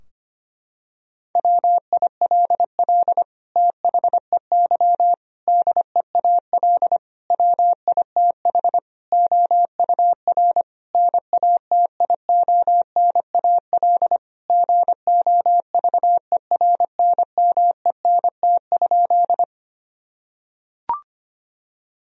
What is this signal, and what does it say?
Morse code, heard as a keyed tone, about 25 wpm, 700 Hz. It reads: WILL THEY DEAL WITH OUR NATIONAL GOVERNMENT?
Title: Sentences from Top 1000 Words - Review Part 4 - 25wpm